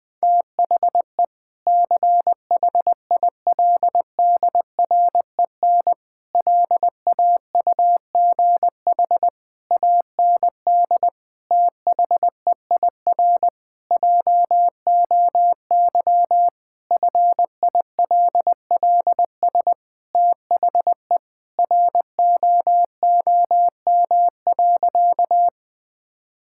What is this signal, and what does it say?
Morse code, heard as a keyed tone, about 20 wpm, 700 Hz. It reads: THE CHILDREN LAUGH AND THEIR JOY FILLS THE ROOM.
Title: Sentences with 2-Word Phrases - 20wpm